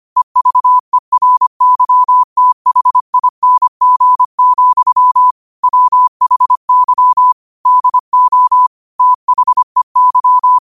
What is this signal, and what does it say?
Morse code, heard as a keyed tone, about 25 wpm, 1000 Hz. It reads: EVERYTHING, WHY DO THEY